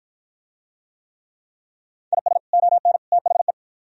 Morse code, heard as a keyed tone, about 47 wpm, 700 Hz. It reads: IS ON THE